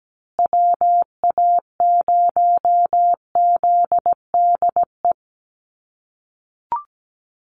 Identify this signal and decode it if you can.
Morse code, heard as a keyed tone, about 17 wpm, 700 Hz. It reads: WA0ZDE
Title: USA Call Signs Repeated 3x 17wpm